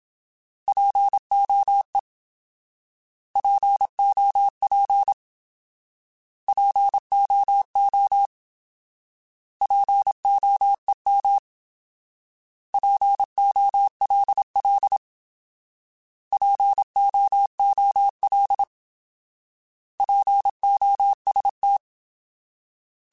Morse code, heard as a keyed tone, about 27 wpm, 800 Hz. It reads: POE POP POO POEM POLL POOL POST
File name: Alphabetical / Alphabetical mp3s / Ps @25